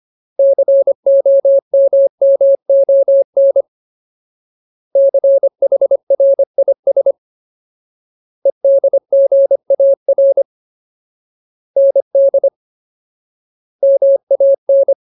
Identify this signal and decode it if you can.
Morse code, heard as a keyed tone, about 25 wpm, 550 Hz. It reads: COMMON CHRIS EDGAR ND MAN
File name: Ultimate Word List / 25WPM Ultimate